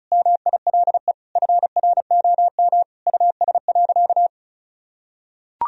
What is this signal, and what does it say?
Morse code, heard as a keyed tone, about 34 wpm, 700 Hz. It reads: MILE FROM US.
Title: Sentences from Top 1000 Words - Review Part 1 - 35wpm